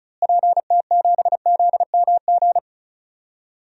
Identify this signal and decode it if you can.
Morse code, heard as a keyed tone, about 36 wpm, 700 Hz. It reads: PT7ZMG
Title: DXCC-100 Call Signs - 6 Characters - Repeated 2x - 35wpm